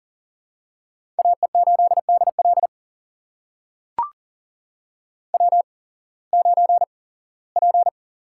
Morse code, heard as a keyed tone, about 39 wpm, spaced as for 21 wpm, 700 Hz. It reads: AE8DL W9P
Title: USA Call Signs - 5 Characters - 8x Character Spacing - 40wpm